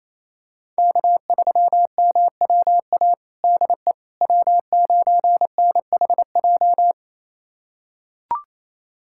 Morse code, heard as a keyed tone, about 28 wpm, 700 Hz. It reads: K3MWA DE W9NHJ
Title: USA Callsign DE Callsign - 5 Characters - 28wpm